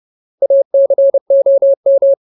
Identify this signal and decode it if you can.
Morse code, heard as a keyed tone, about 30 wpm, 550 Hz. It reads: ACOM
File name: Ultimate Word List / 30WPM Ultimate